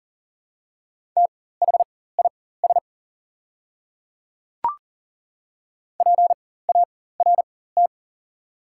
Morse code, heard as a keyed tone, about 40 wpm, spaced as for 19 wpm, 690 Hz. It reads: THIS PART
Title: 4x Character Spacing - Top 200 Words - 40wpm